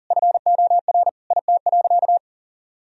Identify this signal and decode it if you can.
Morse code, heard as a keyed tone, about 37 wpm, 700 Hz. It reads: FOR IT.
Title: Sentences from Top 1000 Words - Review Part 2 - 40wpm